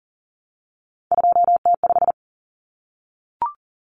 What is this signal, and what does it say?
Morse code, heard as a keyed tone, about 40 wpm, 700 Hz. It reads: 2T5
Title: Common Cut Numbers - 3-Digit 40wpm